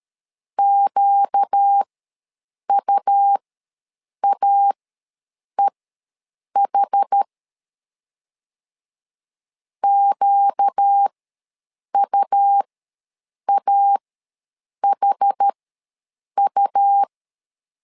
Morse code, heard as a keyed tone, about 13 wpm, spaced as for 7 wpm, 800 Hz. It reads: QUAEH QUAHU
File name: LEZIONI / 4 Q ws 12wpm 1912